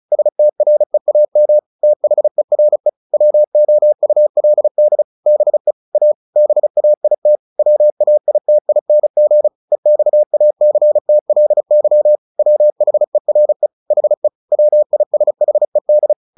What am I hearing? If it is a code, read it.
Morse code, heard as a keyed tone, about 36 wpm, 600 Hz. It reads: STREAM THERE WOULD BE A BAIT WAITING EXACTLY WHERE HE WISHED
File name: the old man and the sea / the old man and the sea 1 35wpm